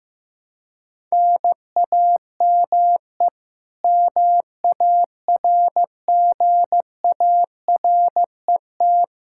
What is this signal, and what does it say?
Morse code, heard as a keyed tone, about 15 wpm, 700 Hz. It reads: NAME MARGARET
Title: K1USN SST QSO Practice 15wpm